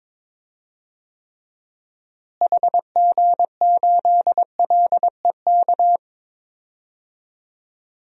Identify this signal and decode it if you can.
Morse code, heard as a keyed tone, about 22 wpm, 700 Hz. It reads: HG8LEK
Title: DXCC-100 Call Signs - 6 Characters - Repeated 2x - 22wpm